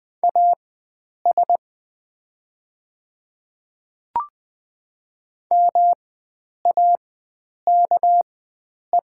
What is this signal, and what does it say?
Morse code, heard as a keyed tone, about 20 wpm, spaced as for 9 wpm, 700 Hz. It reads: AS MAKE
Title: 4x Character Spacing - Top 100 Words - 20wpm